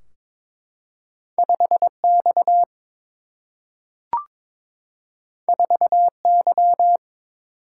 Morse code, heard as a keyed tone, about 22 wpm, 700 Hz. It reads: 5 X 4 Y
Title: Single-Letter Single-Number 22wpm